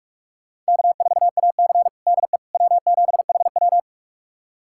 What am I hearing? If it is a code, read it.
Morse code, heard as a keyed tone, about 46 wpm, 700 Hz. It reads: K4AC DE W7HW
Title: USA Callsign DE Callsign - 4 Characters - 45wpm